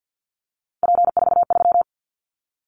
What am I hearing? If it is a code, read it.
Morse code, heard as a keyed tone, about 50 wpm, 700 Hz. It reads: P43
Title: Callsign Prefixes for Top 100 DXCC - 50wpm